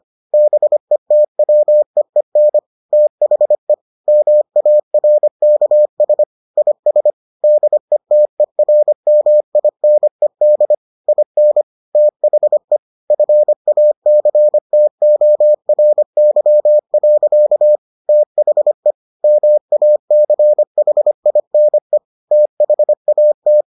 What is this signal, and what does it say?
Morse code, heard as a keyed tone, about 25 wpm, 600 Hz. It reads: BETWEEN THE MARKS IS DETERMINED IN THE FACTORY. THE MACHINE THAT